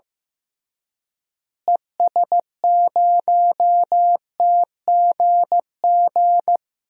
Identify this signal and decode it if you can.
Morse code, heard as a keyed tone, about 15 wpm, 700 Hz. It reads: ES0TGG
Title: DXCC-100 Call Signs - 6 Characters - Repeated 3x - 15wpm